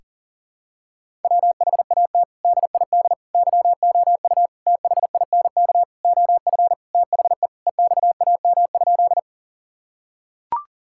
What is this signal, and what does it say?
Morse code, heard as a keyed tone, about 40 wpm, 700 Hz. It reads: WHAT DID YOU THINK OF THE EXAM?
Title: Sentences from Top 1000 Words - Review Part 5 - 40wpm